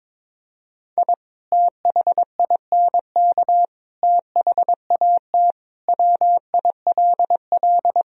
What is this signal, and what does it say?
Morse code, heard as a keyed tone, about 22 wpm, 700 Hz. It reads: I THINK THAT WILL
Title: Sentences from Top 1000 Words - Review Part 2 - 22wpm